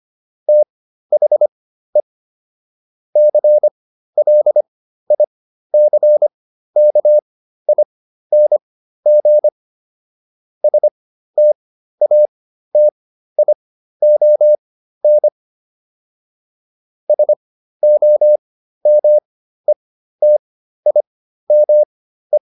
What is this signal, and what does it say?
Morse code, heard as a keyed tone, about 25 wpm, spaced as for 13 wpm, 600 Hz. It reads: THE CLICKING STATION SOMETIME